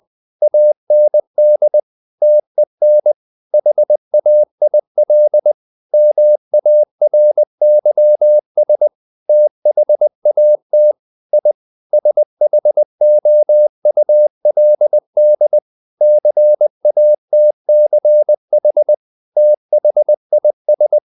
Morse code, heard as a keyed tone, about 20 wpm, 600 Hz. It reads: AND TEN HAIL MARYS THAT I SHOULD CATCH THIS
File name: the old man and the sea / the old man and the sea part 1 20wpm